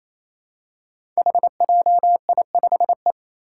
Morse code, heard as a keyed tone, about 28 wpm, 700 Hz. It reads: HJI5E
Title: Five Challenging Letter-Number Combinations 28wpm